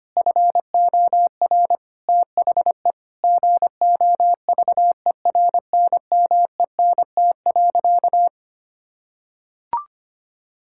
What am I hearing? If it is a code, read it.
Morse code, heard as a keyed tone, about 25 wpm, 700 Hz. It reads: FOR THE GOVERNMENT.
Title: Sentences from Top 1000 Words - Review Part 6 - 25wpm